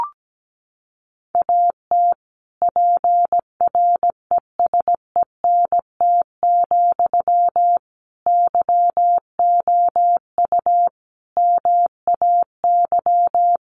Morse code, heard as a keyed tone, about 17 wpm, 700 Hz. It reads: AT PRESENT, YOU MAY
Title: Sentences from Top 800 Words 17wpm